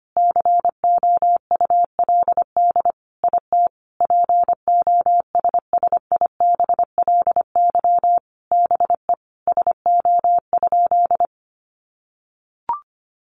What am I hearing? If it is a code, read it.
Morse code, heard as a keyed tone, about 25 wpm, 700 Hz. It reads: COULD IT POSSIBLY BE SO?
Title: Sentences from Top 1300 Words 25wpm